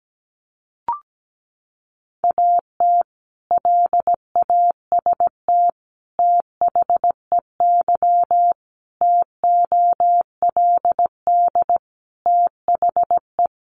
Morse code, heard as a keyed tone, about 17 wpm, 700 Hz. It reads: AT LAST THEY TOLD THE
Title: Sentences from Top 1200 Words 17wpm